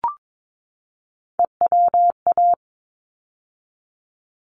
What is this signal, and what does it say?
Morse code, heard as a keyed tone, about 22 wpm, 700 Hz. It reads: EWA
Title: ARRL Sections 22wpm